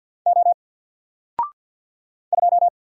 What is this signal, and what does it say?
Morse code, heard as a keyed tone, about 48 wpm, 700 Hz. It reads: O 2
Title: Single Letter-Number Alt 50wpm